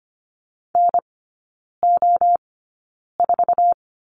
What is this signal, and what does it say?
Morse code, heard as a keyed tone, about 25 wpm, spaced as for 9 wpm, 700 Hz. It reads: NO4
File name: Lesson 14e - Number 2 - Three Characters 25wpm